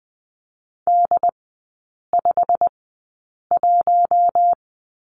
Morse code, heard as a keyed tone, about 20 wpm, spaced as for 8 wpm, 700 Hz. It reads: D51
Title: 087 Lesson 15e - Number 5 - Three Characters 20wpm